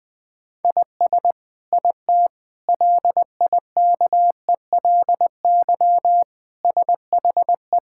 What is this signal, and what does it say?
Morse code, heard as a keyed tone, about 21 wpm, 700 Hz. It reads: IS IT LIKELY SHE
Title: Sentences from Top 1000 Words - Review Part 5 - 20wpm